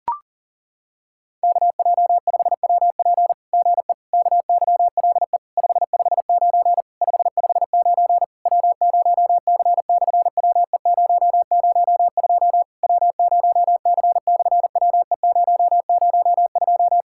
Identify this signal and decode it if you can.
Morse code, heard as a keyed tone, about 40 wpm, 700 Hz. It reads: KJ5WP GE KYLE 559 559 W0C/WE002 W0C/WE002
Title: SOTA QSO Practice 40wpm